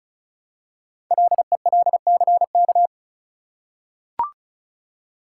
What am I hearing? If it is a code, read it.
Morse code, heard as a keyed tone, about 36 wpm, 700 Hz. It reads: LELCK
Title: Code Groups - 5 Letters - 35wpm